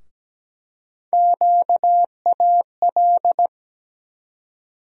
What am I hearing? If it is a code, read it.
Morse code, heard as a keyed tone, about 17 wpm, 700 Hz. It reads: QAL
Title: USA Call Sign Postfixes - 3 Letters 17wpm